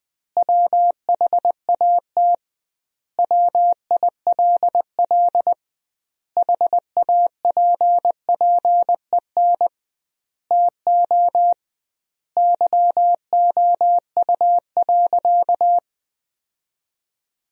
Morse code, heard as a keyed tone, about 21 wpm, 700 Hz. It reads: WHAT WILL HAPPEN TO YOU.
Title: 2x Word-Spacing - Sentences from Top 500 Words - 20wpm